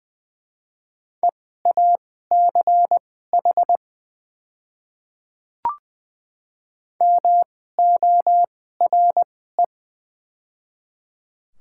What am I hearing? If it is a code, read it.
Morse code, heard as a keyed tone, about 20 wpm, spaced as for 15 wpm, 700 Hz. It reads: EACH MORE
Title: 2x Character Spacing - Top 100 Words - 20wpm